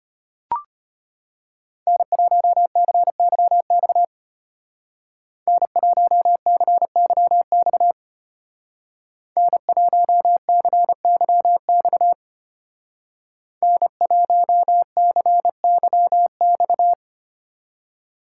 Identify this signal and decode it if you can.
Morse code, heard as a keyed tone, about 33 wpm, 700 Hz. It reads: N1CYX N1CYX N1CYX N1CYX